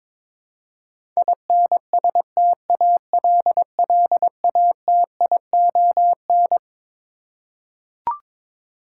Morse code, heard as a keyed tone, about 22 wpm, 700 Hz. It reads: INSTALLATION
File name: Twelve Letter Words - 22wpm